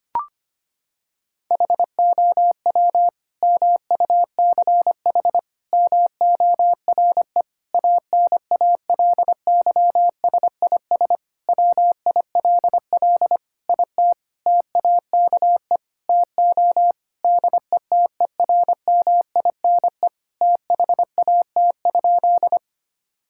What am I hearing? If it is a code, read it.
Morse code, heard as a keyed tone, about 25 wpm, 700 Hz. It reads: HOW MUCH MORE ANALYSIS WILL IT TAKE TO DETERMINE THAT?
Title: Sentences from Top 1000 Words - Review Part 5 - 25wpm